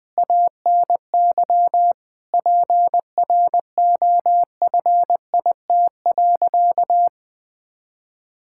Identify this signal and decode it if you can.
Morse code, heard as a keyed tone, about 20 wpm, 700 Hz. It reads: ANY PROFIT.
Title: Sentences from Top 1000 Words - Review Part 5 - 20wpm